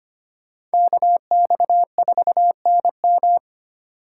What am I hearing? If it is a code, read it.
Morse code, heard as a keyed tone, about 25 wpm, 700 Hz. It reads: KX4NM
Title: K1USN SST QSO Practice - 25wpm